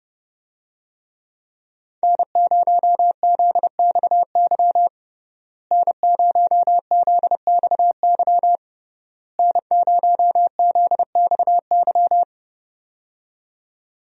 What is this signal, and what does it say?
Morse code, heard as a keyed tone, about 32 wpm, 700 Hz. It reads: N0ZXY N0ZXY N0ZXY
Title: USA Call Signs - 5 Characters - Repeated 3x - 30wpm